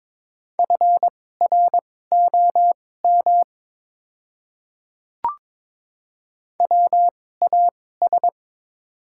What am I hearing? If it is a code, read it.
Morse code, heard as a keyed tone, about 22 wpm, spaced as for 16 wpm, 700 Hz. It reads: FROM WAS
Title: 2x Character Spacing - Top 100 Words - 22wpm